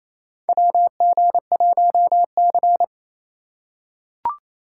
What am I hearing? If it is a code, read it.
Morse code, heard as a keyed tone, about 28 wpm, 700 Hz. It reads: WG1C
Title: USA Call Signs - 4 Characters - 4x Character Spacing - 28wpm